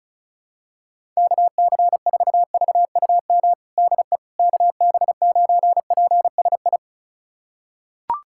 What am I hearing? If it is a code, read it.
Morse code, heard as a keyed tone, about 36 wpm, 700 Hz. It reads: KC4VUM DE KB9PSI
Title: USA Callsign DE Callsign - 6 Characters - 35wpm